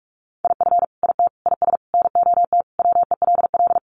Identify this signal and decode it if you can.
Morse code, heard as a keyed tone, about 44 wpm, 700 Hz. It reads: IF IT IS NOT WELL
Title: Sentences from Top 1200 Words 45wpm